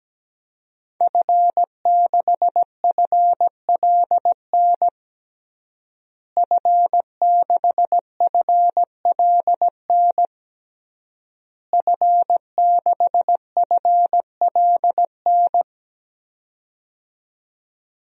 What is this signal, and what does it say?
Morse code, heard as a keyed tone, about 17 wpm, 700 Hz. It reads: F6FLN F6FLN F6FLN
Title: DXCC-100 Call Signs - 5 Characters - Repeated 3x - 17wpm